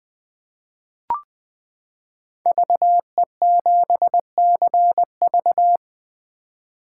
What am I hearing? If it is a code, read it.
Morse code, heard as a keyed tone, about 20 wpm, 700 Hz. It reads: VE7CV
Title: 2023 SOTA Chaser Call Signs - 20wpm